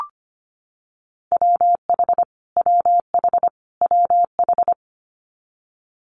Morse code, heard as a keyed tone, about 25 wpm, 700 Hz. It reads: WH WH WH
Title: Multi-letter Phonemes Repeated 3x 25wpm